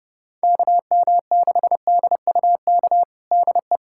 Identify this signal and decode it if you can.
Morse code, heard as a keyed tone, about 31 wpm, 690 Hz. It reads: KM6DUK DE
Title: USA Callsign DE Callsign - 6 Characters - 30wpm